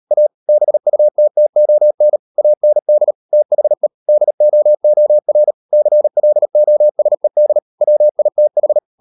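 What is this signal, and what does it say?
Morse code, heard as a keyed tone, about 38 wpm, 600 Hz. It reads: A BUTTON AND THE DOOR CLOSED WITH
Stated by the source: Morse code practice at 38 WPM, sending Ken Follett's novel Never in English, from CW med Gnister.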